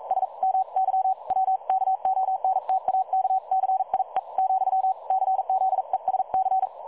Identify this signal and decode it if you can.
Morse code, heard as a keyed tone, about 42 wpm, spaced as for 36 wpm, 760 Hz. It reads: SMY W KONTAKCIE, CZESC
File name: dlazaawansowanych / nagraniazpasma / websdr recording 3565.0kHz